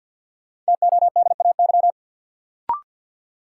Morse code, heard as a keyed tone, about 53 wpm, 700 Hz. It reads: TODAY
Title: Sentences from Top 600 Words - New Word Drill - 50wpm